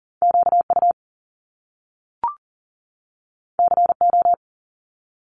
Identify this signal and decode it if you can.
Morse code, heard as a keyed tone, about 39 wpm, 700 Hz. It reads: QU CO
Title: Top Two Letter Combinations in Words 40wpm